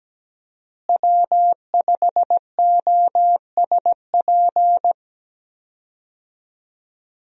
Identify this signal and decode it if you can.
Morse code, heard as a keyed tone, about 17 wpm, 700 Hz. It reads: W5OSP
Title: USA Call Signs - 5 Characters - Repeated 3x - 17wpm